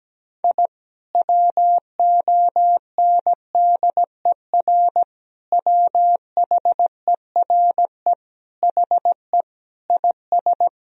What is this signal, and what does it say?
Morse code, heard as a keyed tone, about 17 wpm, 700 Hz. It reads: I WONDER WHERE HE IS
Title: Sentences from Top 1000 Words - Review Part 3 - 17wpm